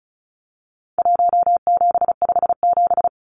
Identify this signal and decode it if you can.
Morse code, heard as a keyed tone, about 34 wpm, 700 Hz. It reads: 1757
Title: Numbers - Contest Style - 4-Digit 35wpm